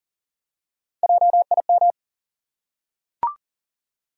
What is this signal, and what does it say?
Morse code, heard as a keyed tone, about 40 wpm, 700 Hz. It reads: JIM
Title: USA Common Names - Max 4 Letters - 40wpm